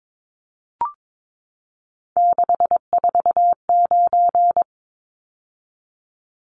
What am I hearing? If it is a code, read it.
Morse code, heard as a keyed tone, about 22 wpm, 700 Hz. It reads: 649
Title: Numbers - 3-Digit 22wpm